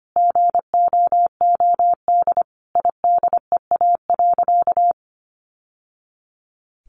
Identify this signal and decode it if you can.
Morse code, heard as a keyed tone, about 25 wpm, 700 Hz. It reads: GOOD IDEA.
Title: Sentences from Top 600 Words 25wpm